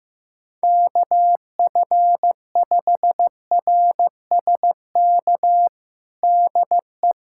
Morse code, heard as a keyed tone, about 15 wpm, 700 Hz. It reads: KF5RSK DE
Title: USA Callsign DE Callsign - 6 Characters - 15wpm